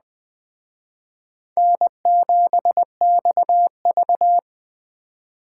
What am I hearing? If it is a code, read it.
Morse code, heard as a keyed tone, about 20 wpm, 700 Hz. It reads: N7XV
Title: Speed Racing - USA Call Signs 17wpm